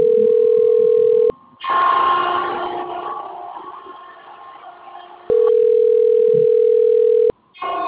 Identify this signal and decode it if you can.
Ringback tone (North America), 440 + 480 Hz.